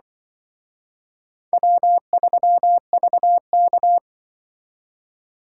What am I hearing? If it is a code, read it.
Morse code, heard as a keyed tone, about 25 wpm, 700 Hz. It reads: W3VK
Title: Speed Racing - USA 1x2 Call Signs 20wpm